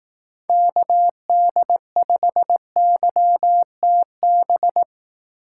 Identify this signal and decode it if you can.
Morse code, heard as a keyed tone, about 18 wpm, 700 Hz. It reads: KD5YTB
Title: Speed Racing - USA 2x3 Call Signs 15wpm